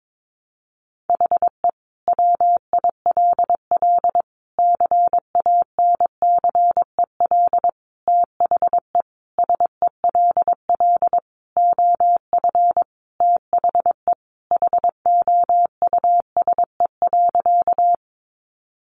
Morse code, heard as a keyed tone, about 22 wpm, 700 Hz. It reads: HE WILL CANCEL THE SELL OF THE HOUSE.
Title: Sentences from Top 1200 Words 22wpm